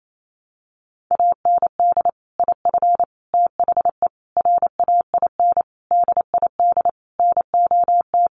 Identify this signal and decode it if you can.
Morse code, heard as a keyed tone, about 28 wpm, 700 Hz. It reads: AND IF THE RAIN DID NOT
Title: Sentences from Top 1300 Words 28wpm